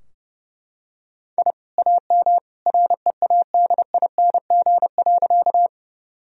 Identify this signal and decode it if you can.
Morse code, heard as a keyed tone, about 32 wpm, 700 Hz. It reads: I AM READING.
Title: Sentences from Top 1000 Words - Review Part 3 - 30wpm